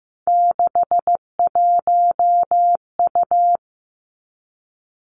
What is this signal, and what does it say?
Morse code, heard as a keyed tone, about 15 wpm, 700 Hz. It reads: 61U
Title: Three Challenging Letter-Number Combinations 15wpm